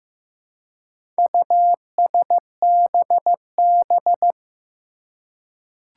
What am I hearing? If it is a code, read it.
Morse code, heard as a keyed tone, about 16 wpm, 700 Hz. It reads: USBB
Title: Four Challenging Letter-Number Combinations 15wpm